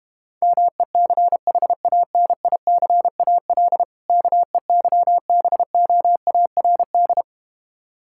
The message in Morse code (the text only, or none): MECHANICAL KEYBOARD